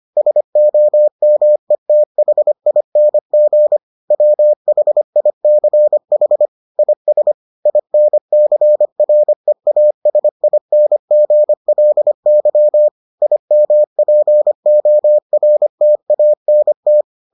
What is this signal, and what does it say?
Morse code, heard as a keyed tone, about 25 wpm, 600 Hz. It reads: SOMETHING WHICH IS INCREASINGLY IMPORTANT